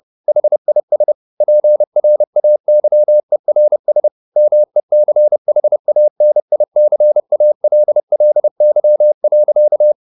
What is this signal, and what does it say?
Morse code, heard as a keyed tone, about 30 wpm, 600 Hz. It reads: HIS PRAYERS MECHANICALLY.
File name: the old man and the sea / the old man and the sea 1 30wpm